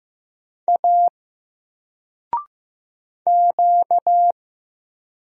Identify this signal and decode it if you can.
Morse code, heard as a keyed tone, about 15 wpm, 700 Hz. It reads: A Q
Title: Single Letter-Number Alt 15wpm